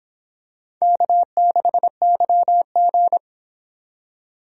Morse code, heard as a keyed tone, about 24 wpm, 700 Hz. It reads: K6YG